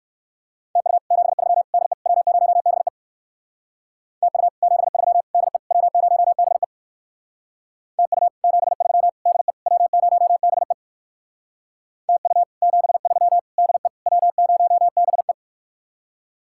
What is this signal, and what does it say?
Morse code, heard as a keyed tone, about 56 wpm, 700 Hz. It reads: TU 73 DE W0BE TU 73 DE W0BE TU 73 DE W0BE TU 73 DE W0BE
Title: Speed-Racing Common QSO Phrases - 45wpm